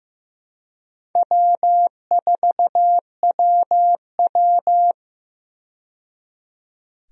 Morse code, heard as a keyed tone, about 15 wpm, 700 Hz. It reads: W4WW
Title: USA Call Signs - 4 Characters - 15wpm